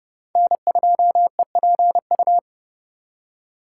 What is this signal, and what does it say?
Morse code, heard as a keyed tone, about 36 wpm, 700 Hz. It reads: N2EPU
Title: Speed Racing - USA 1x3 Call Signs 30wpm